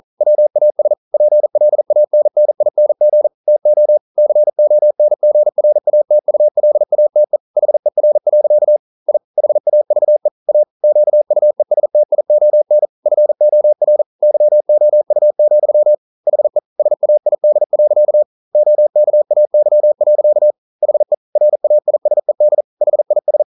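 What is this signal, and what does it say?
Morse code, heard as a keyed tone, about 42 wpm, 600 Hz. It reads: WAS PLANNING TO CONGRATULATE HER. I HAVE A QUESTION FOR YOU, HE SAID. OKAY. HE RAISED HIS
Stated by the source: Morse code practice at 41 wpm, from CW med Gnister.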